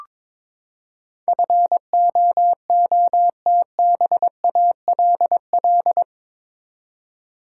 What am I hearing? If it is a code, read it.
Morse code, heard as a keyed tone, about 22 wpm, 700 Hz. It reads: FOOTBALL
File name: practice / 22wpm / Eight Letter Words - 22wpm